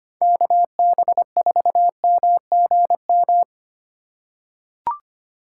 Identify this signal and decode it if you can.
Morse code, heard as a keyed tone, about 25 wpm, 700 Hz. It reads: KB4MGM
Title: USA Call Signs - 6 Characters - 4x Character Spacing - 25wpm